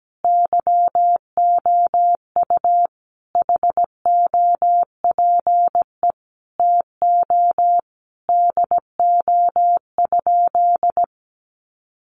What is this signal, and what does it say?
Morse code, heard as a keyed tone, about 17 wpm, 700 Hz. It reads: YOU HOPE TO DO?
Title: Sentences from Top 1000 Words 17wpm